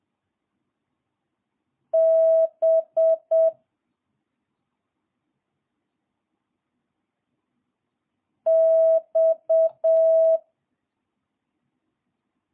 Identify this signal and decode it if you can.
Morse code, heard as a keyed tone, about 7 wpm, 650 Hz. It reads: B X
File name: jh8cbh / 2022cw